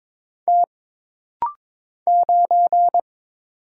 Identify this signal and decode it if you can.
Morse code, heard as a keyed tone, about 22 wpm, 700 Hz. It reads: T 9
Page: This rapid-fire timing is geared to push your ICR (Instant Character Recognition) skills.